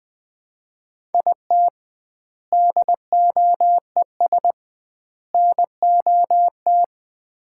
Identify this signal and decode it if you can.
Morse code, heard as a keyed tone, about 20 wpm, 700 Hz. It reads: IT DOES NOT
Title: 2x Word-Spacing - Sets of 3 Words - 20wpm